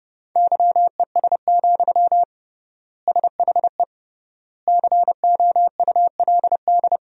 Morse code, heard as a keyed tone, about 32 wpm, 700 Hz. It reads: YES, SHE COULD
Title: 3x Word-Spacing - Sentences from Top 500 Words - 30wpm